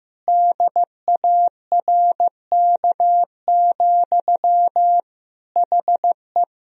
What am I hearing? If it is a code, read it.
Morse code, heard as a keyed tone, about 15 wpm, 700 Hz. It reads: DARK, HE
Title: Sentences from Top 1000 Words - Review Part 4 - 15wpm